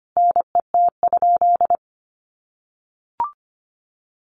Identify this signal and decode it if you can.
Morse code, heard as a keyed tone, about 25 wpm, 700 Hz. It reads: NET?
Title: Sentences from Top 1400 Words 25wpm